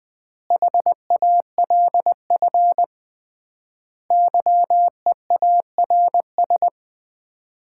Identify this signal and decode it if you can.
Morse code, heard as a keyed tone, about 21 wpm, 700 Hz. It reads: HALF YEARS
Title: 3x Word-Spacing - Sets of 5 Words - 20wpm